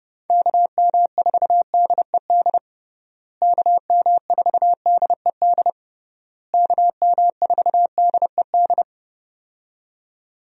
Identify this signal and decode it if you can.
Morse code, heard as a keyed tone, about 30 wpm, 700 Hz. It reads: KM4DED KM4DED KM4DED
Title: USA Call Signs - 6 Characters - Repeated 3x - 30wpm